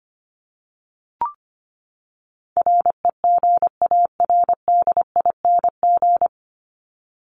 Morse code, heard as a keyed tone, about 25 wpm, 700 Hz. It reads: REGARDING
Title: ING Suffix - Top 100 Words with Base Words 25wpm